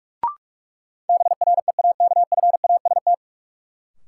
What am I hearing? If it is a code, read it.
Morse code, heard as a keyed tone, about 41 wpm, 700 Hz. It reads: BREAKFAST
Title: Sentences from Top 1500 Words - New Word Drill - 45wpm